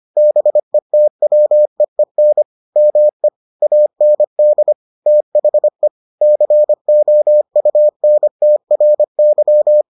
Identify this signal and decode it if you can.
Morse code, heard as a keyed tone, about 25 wpm, 600 Hz. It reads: BETWEEN ME AND THE COUNTRY